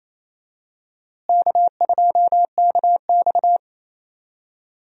Morse code, heard as a keyed tone, about 28 wpm, 700 Hz. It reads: K2KX
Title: USA Call Signs - 4 Characters - Repeated 3x - 28wpm